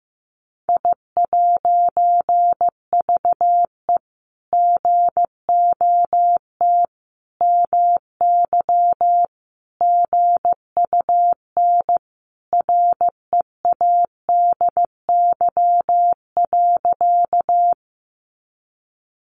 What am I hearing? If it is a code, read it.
Morse code, heard as a keyed tone, about 15 wpm, 700 Hz. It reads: I'VE GOT MY GUN READY.
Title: A Sound Of Thunder 15wpm